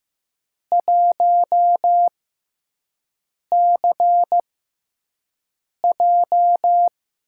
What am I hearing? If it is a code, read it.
Morse code, heard as a keyed tone, about 15 wpm, spaced as for 5 wpm, 700 Hz. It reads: 1CJ
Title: USA Call Signs - 6x Character Spacing - 15wpm